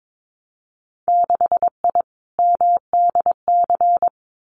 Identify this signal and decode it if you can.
Morse code, heard as a keyed tone, about 22 wpm, 700 Hz. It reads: 6I MDC
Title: Winter Field Day Exchanges Encore 22wpm